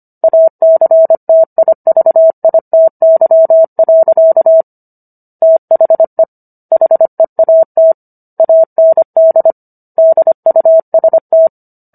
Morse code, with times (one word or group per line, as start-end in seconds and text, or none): 0.2-4.6 ACTIVITY.
5.4-6.2 THE
6.7-7.9 HEAT
8.4-9.5 AND
10.0-11.5 DUST